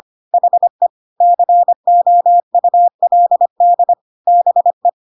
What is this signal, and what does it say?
Morse code, heard as a keyed tone, about 25 wpm, 700 Hz. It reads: HE COULD BE
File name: Pinocchio / Pinocchio ch7 25 25 ik1hge